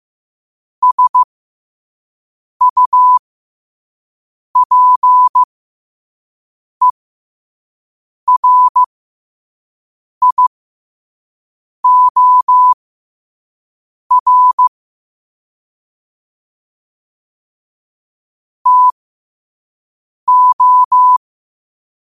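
Morse code, heard as a keyed tone, about 15 wpm, spaced as for 5 wpm, 1000 Hz. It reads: SUPERIOR TO
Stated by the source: Morse code at 5 words per minute.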